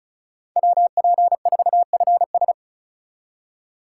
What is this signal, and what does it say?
Morse code, heard as a keyed tone, about 47 wpm, 700 Hz. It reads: WP4FS